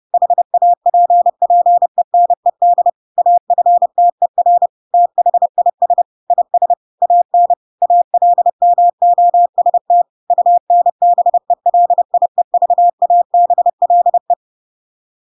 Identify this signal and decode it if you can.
Morse code, heard as a keyed tone, about 30 wpm, 700 Hz. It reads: HAPPENED AFTER THIS IS AN ALMOST UNBELIEVABLE
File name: Pinocchio / Pinocchio ch3 30 30 ik1hge